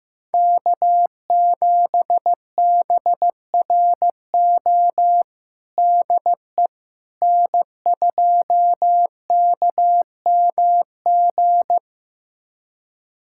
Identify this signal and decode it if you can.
Morse code, heard as a keyed tone, about 15 wpm, 700 Hz. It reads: K7BRO DE N2KMG
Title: USA Callsign DE Callsign - 5 Characters - 15wpm